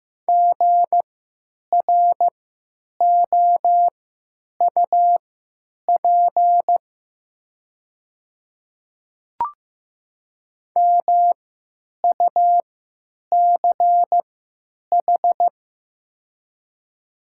Morse code, heard as a keyed tone, about 15 wpm, spaced as for 9 wpm, 700 Hz. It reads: GROUP MUCH
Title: 3x Character Spacing - Top 300 Words - 15wpm